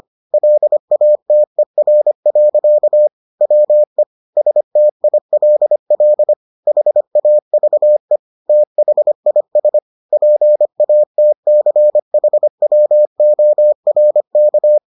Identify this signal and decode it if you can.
Morse code, heard as a keyed tone, about 25 wpm, 600 Hz. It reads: LATER. WE STILL HAVE THIS PATCHWORK